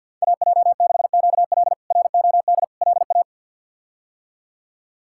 Morse code, heard as a keyed tone, about 52 wpm, 690 Hz. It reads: AJ6QL ROD LA
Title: K1USN SST QSO Practice - 50wpm